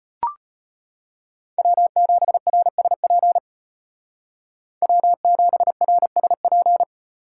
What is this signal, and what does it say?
Morse code, heard as a keyed tone, about 32 wpm, 700 Hz. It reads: W7RSP W7RSP